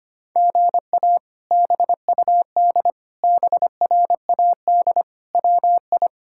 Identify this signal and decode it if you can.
Morse code, heard as a keyed tone, about 25 wpm, 700 Hz. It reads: GA BUD BRAD WI
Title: K1USN SST QSO Practice - 25wpm